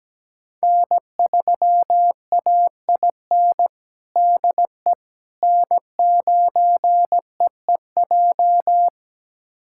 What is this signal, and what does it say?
Morse code, heard as a keyed tone, about 17 wpm, 700 Hz. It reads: N3AIN DE N9EEJ
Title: USA Callsign DE Callsign 17wpm